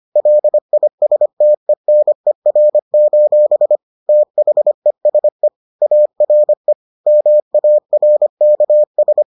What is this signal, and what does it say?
Morse code, heard as a keyed tone, about 25 wpm, 600 Hz. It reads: LISTENER: THESE ARE MARKS